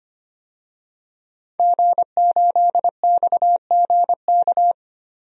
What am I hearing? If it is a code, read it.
Morse code, heard as a keyed tone, about 25 wpm, 700 Hz. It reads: G8XGK
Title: DXCC-100 Call Signs - 5 Characters - 25wpm